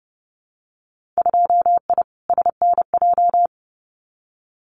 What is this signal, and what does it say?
Morse code, heard as a keyed tone, about 30 wpm, 700 Hz. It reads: 2I SNJ
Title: Winter Field Day Exchanges Encore 30wpm